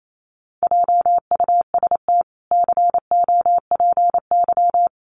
Morse code, heard as a keyed tone, about 28 wpm, 700 Hz. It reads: JUST COPY
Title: Sentences from Top 1400 Words 28wpm